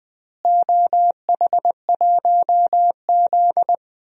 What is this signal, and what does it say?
Morse code, heard as a keyed tone, about 21 wpm, 700 Hz. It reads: OH1Z
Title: DXCC-100 Call Signs - 4 Characters - Repeated 2x - 20wpm